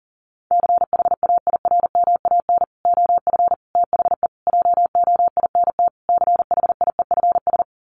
Morse code, heard as a keyed tone, about 40 wpm, 700 Hz. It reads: CHAIRMAN OF THE JOINT CHIEFS
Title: Sets of 5 Words Encore 40wpm